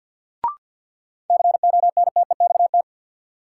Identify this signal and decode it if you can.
Morse code, heard as a keyed tone, about 52 wpm, 700 Hz. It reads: CONTEXT
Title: Sets of 4 Words - New Word Drill - 50wpm